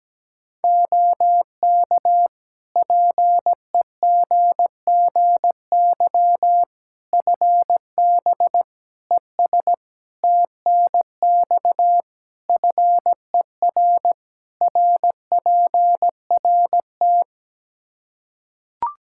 Morse code, heard as a keyed tone, about 17 wpm, 700 Hz. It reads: OK PEGGY FB ES TNX FER RPRT
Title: Practice QSO - Exchange 2 - Info - 17wpm